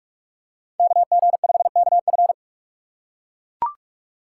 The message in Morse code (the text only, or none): KG5KF